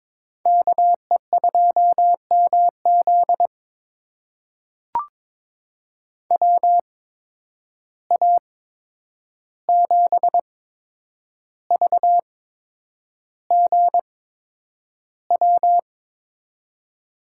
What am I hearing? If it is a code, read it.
Morse code, heard as a keyed tone, about 21 wpm, spaced as for 9 wpm, 700 Hz. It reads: KE2MZ WA7VGW
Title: USA Call Signs - 8x Character Spacing - 22wpm